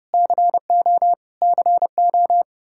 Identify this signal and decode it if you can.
Morse code, heard as a keyed tone, about 30 wpm, 700 Hz. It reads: CO CO
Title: Practice QSO - Exchange 1 - Beginning - 30wpm